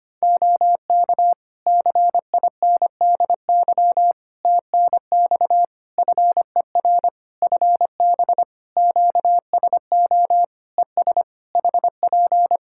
Morse code, heard as a keyed tone, about 25 wpm, 700 Hz. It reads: OK CINDY TNX FER FB QSO ES HP
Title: K1USN SST QSO Practice 25wpm